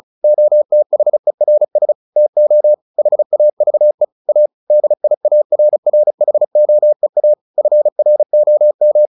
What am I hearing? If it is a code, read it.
Morse code, heard as a keyed tone, about 34 wpm, 600 Hz. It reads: OTHERS TO HAVE A DIARRHOEA FROM
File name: the old man and the sea / the old man and the sea 1 35wpm